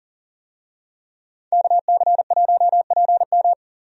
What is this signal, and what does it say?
Morse code, heard as a keyed tone, about 40 wpm, 700 Hz. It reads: KC1PM
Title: USA Call Signs - 5 Characters - Repeated 2x - 40wpm